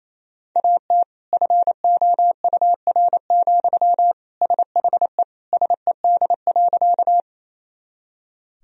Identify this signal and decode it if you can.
Morse code, heard as a keyed tone, about 28 wpm, 700 Hz. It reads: AT FOUR, SHE SED.
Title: Sentences from Top 300 Words and Abbreviations 28wpm